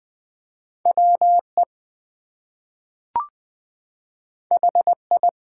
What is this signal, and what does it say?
Morse code, heard as a keyed tone, about 20 wpm, 700 Hz. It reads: WE HI
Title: Two Letter Words - 20wpm